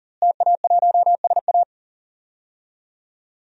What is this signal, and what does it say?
Morse code, heard as a keyed tone, about 40 wpm, 700 Hz. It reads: TA1SA
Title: DXCC-100 Call Signs - 5 Characters - Repeated 2x - 40wpm